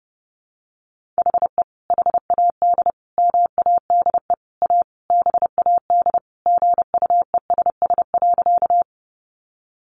Morse code, heard as a keyed tone, about 28 wpm, 700 Hz. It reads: HE HAD MADE A BAD GUESS.
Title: Sentences from Top 1500 Words 30wpm